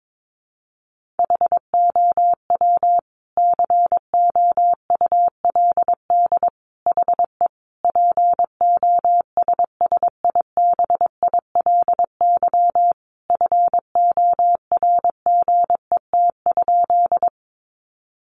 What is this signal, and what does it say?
Morse code, heard as a keyed tone, about 22 wpm, 700 Hz. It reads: HOW COULD HE POSSIBILY FORGET?
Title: Sentences from Top 1300 Words 22wpm